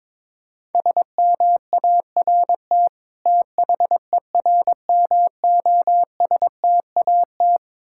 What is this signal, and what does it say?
Morse code, heard as a keyed tone, about 22 wpm, 700 Hz. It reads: SMART THERMOSTAT
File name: Difficult Words with Hints - 22wpm